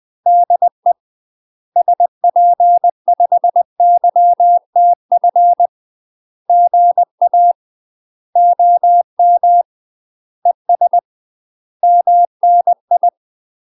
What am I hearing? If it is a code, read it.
Morse code, heard as a keyed tone, about 20 wpm, 700 Hz. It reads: DE SP5YTF GA OM ES MNI
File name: UK MASTER QSOs TRAINING / 20WPM / UK MASTER QSOs 0046